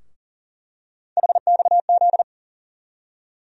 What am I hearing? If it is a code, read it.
Morse code, heard as a keyed tone, about 40 wpm, 700 Hz. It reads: HXZ
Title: USA Call Sign Postfixes - 3 Letters 40wpm